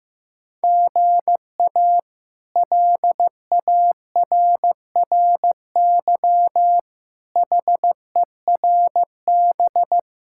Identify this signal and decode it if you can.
Morse code, heard as a keyed tone, about 15 wpm, 700 Hz. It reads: GA LARRY HERB
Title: K1USN SST QSO Practice - 15wpm